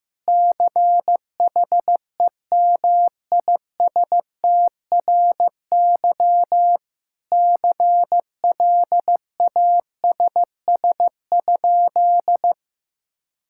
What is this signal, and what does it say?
Morse code, heard as a keyed tone, about 15 wpm, 700 Hz. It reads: CHEMISTRY CLASS?
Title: Sentences from Top 1000 Words - Review Part 6 - 15wpm